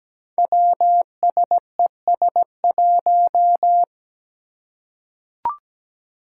Morse code, heard as a keyed tone, about 17 wpm, 700 Hz. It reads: WSES1
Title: Five Challenging Letter-Number Combinations 17wpm